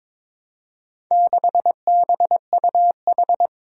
Morse code, heard as a keyed tone, about 22 wpm, 700 Hz. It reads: 6BUH
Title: Four Challenging Letter-Number Combinations 22wpm